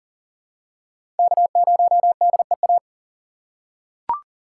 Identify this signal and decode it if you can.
Morse code, heard as a keyed tone, about 40 wpm, 700 Hz. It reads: K0DEA
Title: USA Call Signs - 5 Characters - 4x Character Spacing - 40wpm